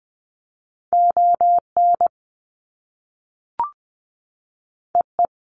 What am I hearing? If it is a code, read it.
Morse code, heard as a keyed tone, about 20 wpm, 700 Hz. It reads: ON EE
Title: Top Two Letter Combinations in Words 20wpm